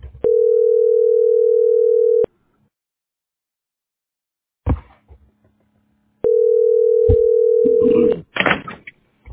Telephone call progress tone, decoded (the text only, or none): ringback tone